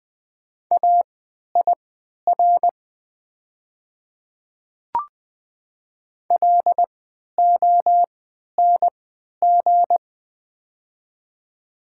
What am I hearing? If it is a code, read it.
Morse code, heard as a keyed tone, about 20 wpm, spaced as for 11 wpm, 690 Hz. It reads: AIR LONG
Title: 3x Character Spacing - Top 500 Words - 20wpm